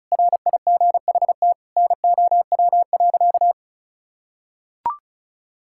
Morse code, heard as a keyed tone, about 36 wpm, 700 Hz. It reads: RIGHT NOW.
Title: Sentences from Top 1600 Words 35wpm